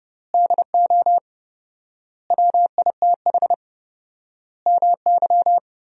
Morse code, heard as a keyed tone, about 30 wpm, 700 Hz. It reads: DO WITH MY
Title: 4x Word-Spacing - Sentences from Top 300 Words - 30wpm